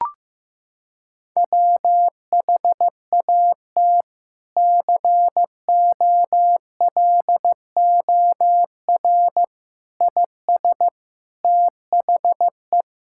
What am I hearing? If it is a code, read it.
Morse code, heard as a keyed tone, about 15 wpm, 700 Hz. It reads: WHAT COLOR IS THE
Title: Sentences from Top 1000 Words - Review Part 6 - 15wpm